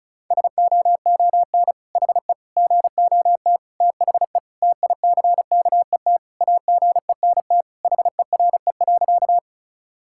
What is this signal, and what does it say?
Morse code, heard as a keyed tone, about 36 wpm, 700 Hz. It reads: SOON HE GOT THE TICKET AGENT HERE.
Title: Sentences from Top 1600 Words 35wpm